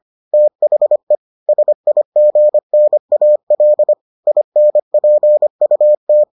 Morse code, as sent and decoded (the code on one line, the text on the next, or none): - .... / ... .. --. -. .- .-.. / .. -. .--. ..- -
THE SIGNAL INPUT